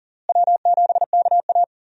Morse code, heard as a keyed tone, about 40 wpm, 700 Hz. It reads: W7KA
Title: USA Call Signs - 4 Characters - Repeated 2x - 40wpm